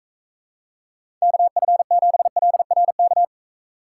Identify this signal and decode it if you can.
Morse code, heard as a keyed tone, about 43 wpm, 700 Hz. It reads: KF7LRK